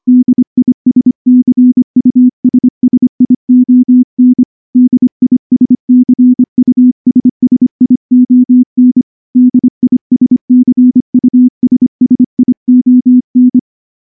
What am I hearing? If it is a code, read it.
Morse code, heard as a keyed tone, about 24 wpm, 250 Hz. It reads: DISCUSSION DISCUSSION DISCUSSION
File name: morse / morse-training-25WPM